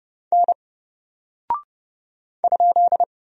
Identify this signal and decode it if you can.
Morse code, heard as a keyed tone, about 30 wpm, 700 Hz. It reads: N ?
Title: Single Character (Letter:Number:Punctuation:Prosign) 30wpm